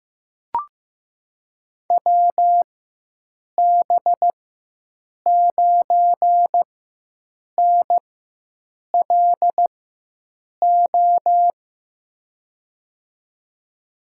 Morse code, heard as a keyed tone, about 15 wpm, spaced as for 7 wpm, 700 Hz. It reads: WB9NLO
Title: USA Call Signs - 6 Characters - 4x Character Spacing - 15wpm